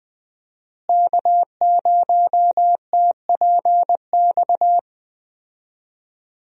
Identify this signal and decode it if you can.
Morse code, heard as a keyed tone, about 20 wpm, 700 Hz. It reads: K0TPX
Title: USA Call Signs - 5 Characters - Repeated 2x - 20wpm